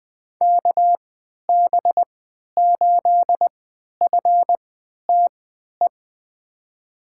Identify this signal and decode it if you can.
Morse code, heard as a keyed tone, about 20 wpm, spaced as for 11 wpm, 700 Hz. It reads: KB8FTE
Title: USA Call Signs - 3x Character Spacing - 20wpm